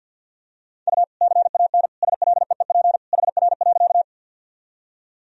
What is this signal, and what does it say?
Morse code, heard as a keyed tone, about 52 wpm, 700 Hz. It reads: U CAN SLEEP HR.
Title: Sentences from Top 500 Words and Abbreviations 50wpm